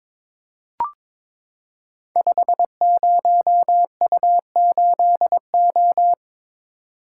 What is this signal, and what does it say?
Morse code, heard as a keyed tone, about 22 wpm, 700 Hz. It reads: 50U8O
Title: Code Groups - 5 Letter:Numbers - 22wpm